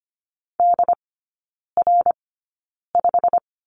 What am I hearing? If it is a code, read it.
Morse code, heard as a keyed tone, about 25 wpm, spaced as for 9 wpm, 690 Hz. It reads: DR5
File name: Lesson 15e - Number 5 - Three Characters 25wpm